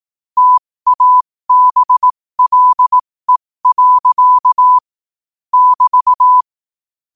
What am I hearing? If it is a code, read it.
Morse code, heard as a keyed tone, about 18 wpm, spaced as for 15 wpm, 1000 Hz. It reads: TABLE. <BT>
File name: qotdfiles / 15wpm